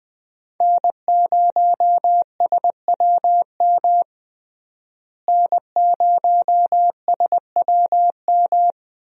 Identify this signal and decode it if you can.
Morse code, heard as a keyed tone, about 21 wpm, 700 Hz. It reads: N0SWM N0SWM
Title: USA Call Signs - 5 Characters - Repeated 2x - 20wpm